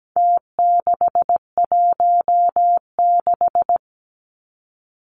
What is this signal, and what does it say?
Morse code, heard as a keyed tone, about 17 wpm, 700 Hz. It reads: T616
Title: Numbers - Contest Style - 4-Digit 17wpm